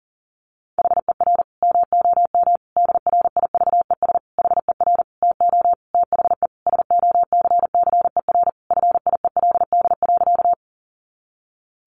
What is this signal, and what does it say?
Morse code, heard as a keyed tone, about 37 wpm, 700 Hz. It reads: HER MOM DRIVES HER TO THE SOCCER FIELD.
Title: Easy Phrases 40wpm